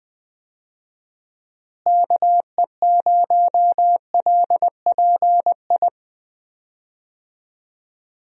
Morse code, heard as a keyed tone, about 19 wpm, 700 Hz. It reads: KE0LPI
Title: USA Call Signs - 6 Characters - Repeated 3x - 20wpm